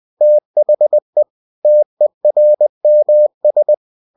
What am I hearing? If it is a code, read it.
Morse code, heard as a keyed tone, about 20 wpm, 600 Hz. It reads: THE TERMS
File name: cw-books / winnie-the-pooh / winnie-20wpm-page-0113